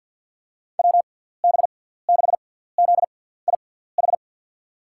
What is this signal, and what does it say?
Morse code, heard as a keyed tone, about 46 wpm, spaced as for 17 wpm, 700 Hz. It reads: WB6ZIH
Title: USA Call Signs - 6 Characters - 6x Character Spacing - 50wpm